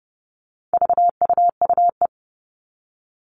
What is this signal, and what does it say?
Morse code, heard as a keyed tone, about 30 wpm, 700 Hz. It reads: VUUE
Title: All Cut Numbers - 4-Digit 30wpm